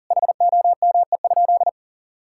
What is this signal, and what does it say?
Morse code, heard as a keyed tone, about 37 wpm, 690 Hz. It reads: HOME?
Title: Sentences from Top 1000 Words - Review Part 6 - 40wpm